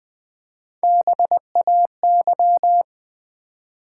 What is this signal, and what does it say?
Morse code, heard as a keyed tone, about 20 wpm, 700 Hz. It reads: BAY
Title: Build A Word - 1 Letter At A Time - 3-Letter Words - 20wpm